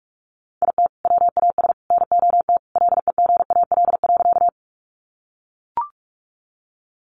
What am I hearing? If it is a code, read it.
Morse code, heard as a keyed tone, about 44 wpm, 700 Hz. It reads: IT WAS NOT LEGAL.
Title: Sentences from Top 1200 Words 45wpm